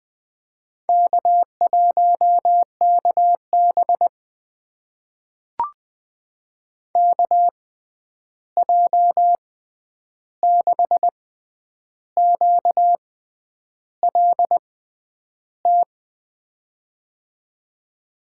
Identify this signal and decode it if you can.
Morse code, heard as a keyed tone, about 21 wpm, spaced as for 9 wpm, 700 Hz. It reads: K1KB KJ6QLT
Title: USA Call Signs - 6x Character Spacing - 20wpm